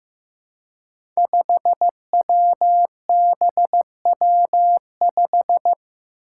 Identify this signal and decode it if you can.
Morse code, heard as a keyed tone, about 15 wpm, 700 Hz. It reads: 5WBW5
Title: Five Challenging Letter-Number Combinations 15wpm